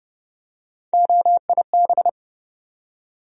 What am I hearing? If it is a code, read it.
Morse code, heard as a keyed tone, about 30 wpm, 700 Hz. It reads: OIB
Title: USA Call Sign Postfixes - 3 Letters 30wpm